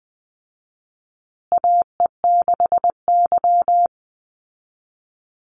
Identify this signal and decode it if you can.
Morse code, heard as a keyed tone, about 20 wpm, 700 Hz. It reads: AE6Y
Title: Call Signs Repeated 2x - 20wpm